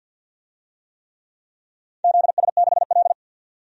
Morse code, heard as a keyed tone, about 49 wpm, 700 Hz. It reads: ZS6L
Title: DXCC-100 Call Signs - 4 Characters - 50wpm